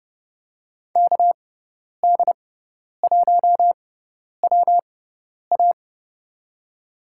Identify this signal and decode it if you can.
Morse code, heard as a keyed tone, about 32 wpm, spaced as for 10 wpm, 700 Hz. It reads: KD1WA
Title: USA Call Signs - 5 Characters - 6x Character Spacing - 30wpm